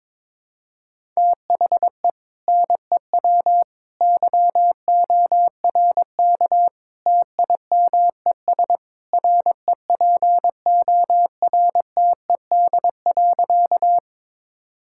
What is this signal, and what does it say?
Morse code, heard as a keyed tone, about 22 wpm, 700 Hz. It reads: THE NEW YORK TIMES REPORTED.
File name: practice / 22wpm / Sentences with 3-Word Phrases - 22wpm